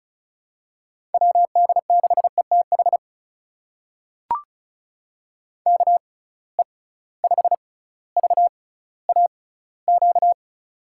Morse code, heard as a keyed tone, about 36 wpm, spaced as for 18 wpm, 700 Hz. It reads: WD6ETH KE5VAQ
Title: USA Call Signs - 6 Characters - 6x Character Spacing - 35wpm